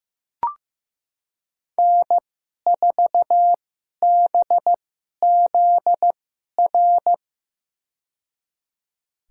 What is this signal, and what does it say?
Morse code, heard as a keyed tone, about 15 wpm, spaced as for 11 wpm, 700 Hz. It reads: N4BZR
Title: USA Call Signs - 5 Characters - 2x Character Spacing - 15wpm